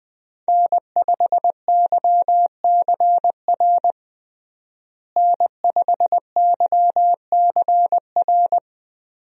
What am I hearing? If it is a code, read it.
Morse code, heard as a keyed tone, about 21 wpm, 700 Hz. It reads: N5YCR N5YCR
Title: USA Call Signs - 5 Characters - Repeated 2x - 20wpm